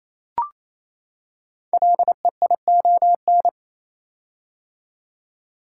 Morse code, heard as a keyed tone, about 29 wpm, 700 Hz. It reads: LEION
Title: Code Groups - 5 Letter:Numbers - 28wpm